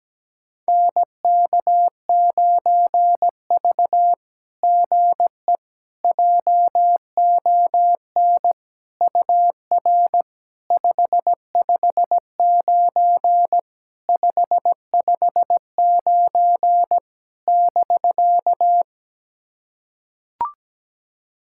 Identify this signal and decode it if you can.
Morse code, heard as a keyed tone, about 17 wpm, 700 Hz. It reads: NK9V GE JON UR 559 559 <BK>
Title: SOTA QSO Practice 17wpm